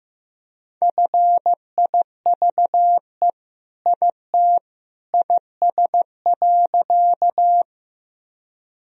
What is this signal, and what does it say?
Morse code, heard as a keyed tone, about 16 wpm, 700 Hz. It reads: FIVE IT IS.
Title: Sentences from Top 1000 Words - Review Part 2 - 15wpm